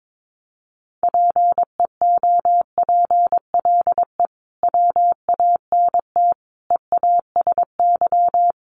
Morse code, heard as a keyed tone, about 22 wpm, 700 Hz. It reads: PEOPLE WANT EASY
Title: Sentences from Top 1500 Words 22wpm